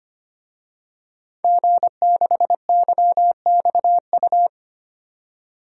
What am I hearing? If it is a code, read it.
Morse code, heard as a keyed tone, about 25 wpm, 700 Hz. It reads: G6YXU
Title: DXCC-100 Call Signs - 5 Characters - 25wpm